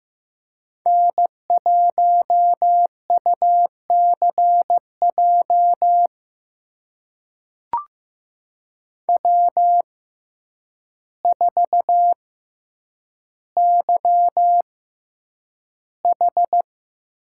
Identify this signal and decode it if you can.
Morse code, heard as a keyed tone, about 15 wpm, spaced as for 8 wpm, 700 Hz. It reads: N1UCJ W4YH
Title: USA Call Signs - 5 Characters - 6x Character Spacing - 15wpm